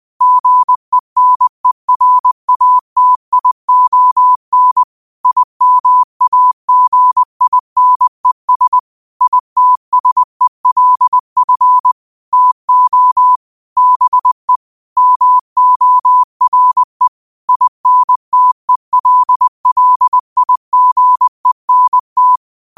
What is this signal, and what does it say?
Morse code, heard as a keyed tone, about 20 wpm, 1000 Hz. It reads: GENERATION IMAGINES ITSELF TO BE MORE INTELLIGENT
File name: qotdfiles / 20wpm